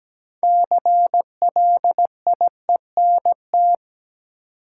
Morse code, heard as a keyed tone, about 17 wpm, 700 Hz. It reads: CLIENT
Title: Six Letter Words - 17wpm